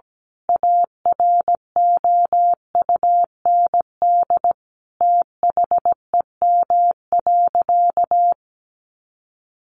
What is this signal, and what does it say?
Morse code, heard as a keyed tone, about 17 wpm, 700 Hz. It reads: AROUND THEM.
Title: A Sound Of Thunder 17wpm